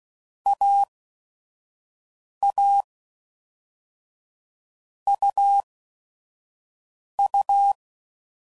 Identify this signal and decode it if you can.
Morse code, heard as a keyed tone, about 16 wpm, 800 Hz. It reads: A A U U
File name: Basic Letter Groups By Blocks / Groups@5 / AWUNDG@5 f15